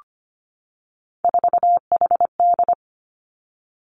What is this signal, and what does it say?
Morse code, heard as a keyed tone, about 25 wpm, 700 Hz. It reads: 4HD
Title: Three Challenging Letter-Number Combinations 25wpm